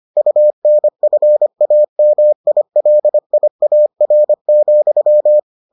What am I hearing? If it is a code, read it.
Morse code, heard as a keyed tone, about 25 wpm, 600 Hz. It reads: UNFAMILIAR,